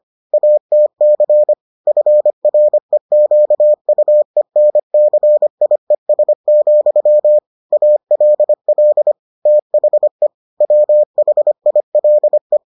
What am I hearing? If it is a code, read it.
Morse code, heard as a keyed tone, about 25 wpm, 600 Hz. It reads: ATC FREQUENCIES, ALL THE WHILE